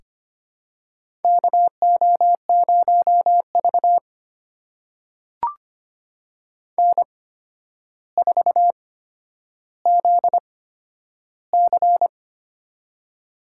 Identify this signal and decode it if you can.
Morse code, heard as a keyed tone, about 25 wpm, spaced as for 11 wpm, 700 Hz. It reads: KO0V N4ZC
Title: USA Call Signs - 4 Characters - 8x Character Spacing - 25wpm